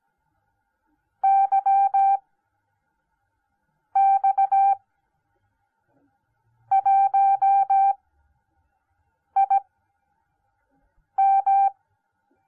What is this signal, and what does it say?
Morse code, heard as a keyed tone, about 17 wpm, spaced as for 5 wpm, 780 Hz. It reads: YX1IM